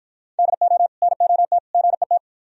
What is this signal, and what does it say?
Morse code, heard as a keyed tone, about 51 wpm, 700 Hz. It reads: DO NOT GET